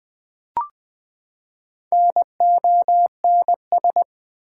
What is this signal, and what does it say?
Morse code, heard as a keyed tone, about 20 wpm, 700 Hz. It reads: NONS
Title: Build A Word - 4 Letters At A Time - 8-Letter Words - 20wpm